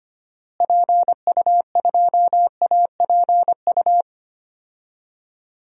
Morse code, heard as a keyed tone, about 25 wpm, 700 Hz. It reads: PU2APU
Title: DXCC-100 Call Signs - 6 Characters - Repeated 2x - 25wpm